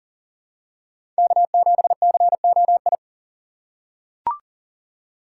Morse code, heard as a keyed tone, about 40 wpm, 700 Hz. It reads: K7COI
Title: USA Call Signs - 5 Characters - 6x Character Spacing - 40wpm